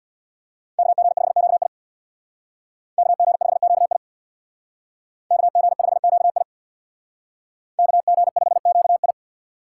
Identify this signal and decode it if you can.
Morse code, heard as a keyed tone, about 60 wpm, 700 Hz. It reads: KG5QI KG5QI KG5QI KG5QI
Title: Speed Racing - USA 2x2 Call Signs 50wpm